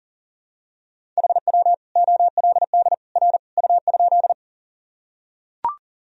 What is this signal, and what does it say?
Morse code, heard as a keyed tone, about 40 wpm, 700 Hz. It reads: HW OLD R U?
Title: Sentences from Top 200 Words and Abbreviations 40wpm